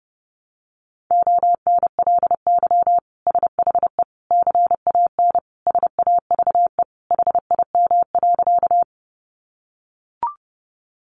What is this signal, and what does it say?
Morse code, heard as a keyed tone, about 30 wpm, 700 Hz. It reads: ONLY SHE CAN SAVE HIM.
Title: Sentences from Top 1100 Words 30wpm